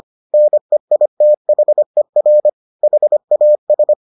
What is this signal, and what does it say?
Morse code, heard as a keyed tone, about 25 wpm, 600 Hz. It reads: NEITHER HAS